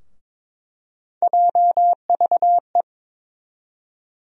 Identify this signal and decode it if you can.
Morse code, heard as a keyed tone, about 22 wpm, 700 Hz. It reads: JVE
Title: USA Call Sign Postfixes - 3 Letters 22wpm